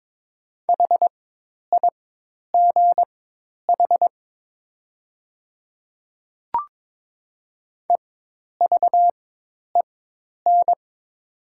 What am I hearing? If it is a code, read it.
Morse code, heard as a keyed tone, about 22 wpm, spaced as for 10 wpm, 700 Hz. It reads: HIGH EVEN
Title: 4x Character Spacing - Top 300 Words - 22wpm